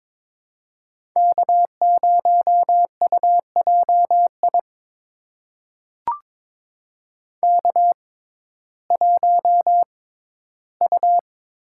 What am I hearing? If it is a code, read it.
Morse code, heard as a keyed tone, about 22 wpm, spaced as for 14 wpm, 700 Hz. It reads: K0UJI K1U
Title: USA Call Signs - 5 Characters - 6x Character Spacing - 22wpm